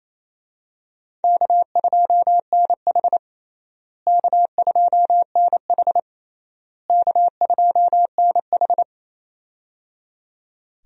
Morse code, heard as a keyed tone, about 27 wpm, 700 Hz. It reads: K2NH K2NH K2NH
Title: USA Call Signs - 4 Characters - Repeated 3x - 28wpm